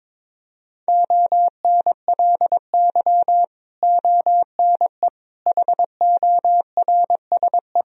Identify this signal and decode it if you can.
Morse code, heard as a keyed tone, about 22 wpm, 700 Hz. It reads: ONLY ONE HORSE
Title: Sentences from Top 1000 Words - Review Part 1 - 22wpm